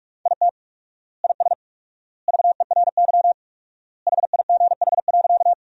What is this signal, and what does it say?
Morse code, heard as a keyed tone, about 47 wpm, 700 Hz. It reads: IT IS VERY HIGH.
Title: 4x Word-Spacing - Sentences from Top 200 Words - 45wpm